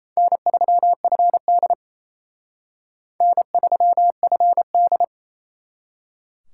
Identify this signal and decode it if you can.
Morse code, heard as a keyed tone, about 34 wpm, 700 Hz. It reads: N3FD N3FD
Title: Speed Racing - USA 1x2 Call Signs 28wpm